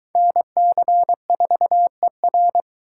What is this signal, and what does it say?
Morse code, heard as a keyed tone, about 26 wpm, 700 Hz. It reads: NC4ER